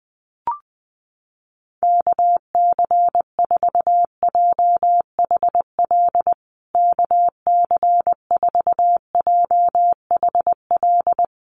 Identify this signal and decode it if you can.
Morse code, heard as a keyed tone, about 20 wpm, 700 Hz. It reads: KC4JHL KC4JHL
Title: USA Call Signs Repeated 2x 20wpm